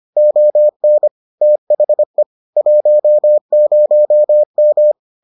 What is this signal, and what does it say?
Morse code, heard as a keyed tone, about 25 wpm, 600 Hz. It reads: ON THE 10M